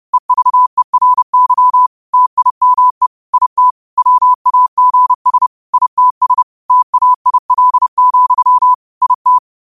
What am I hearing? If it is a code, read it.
Morse code, heard as a keyed tone, about 30 wpm, 1000 Hz. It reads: EVERY TIME IT WAGS ITS TAIL, IT
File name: qotdfiles / 30wpm